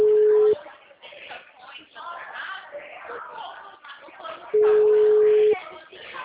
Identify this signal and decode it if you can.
Ringback tone (Europe, 425 Hz), with speech.
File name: out- -2010- - - .55385